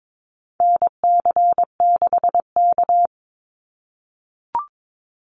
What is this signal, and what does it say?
Morse code, heard as a keyed tone, about 22 wpm, 700 Hz. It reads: NC6K
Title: Call Signs - 22wpm